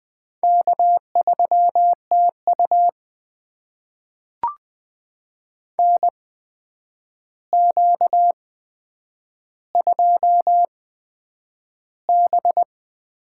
Morse code, heard as a keyed tone, about 20 wpm, spaced as for 9 wpm, 700 Hz. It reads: K3TU NQ2B
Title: USA Call Signs - 8x Character Spacing - 20wpm